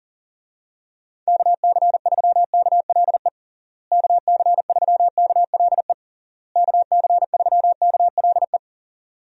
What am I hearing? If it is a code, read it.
Morse code, heard as a keyed tone, about 40 wpm, 700 Hz. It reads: KC3KLE KC3KLE KC3KLE
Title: USA Call Signs - 6 Characters - Repeated 3x - 40wpm